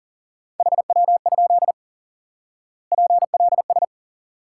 Morse code, heard as a keyed tone, about 40 wpm, 700 Hz. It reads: HW? PLS
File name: IQE Course 3 - Lesson 89 - YAGI - Two Elements - 40wpm